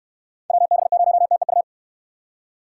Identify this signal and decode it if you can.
Morse code, heard as a keyed tone, about 69 wpm, 700 Hz. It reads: WB0TEU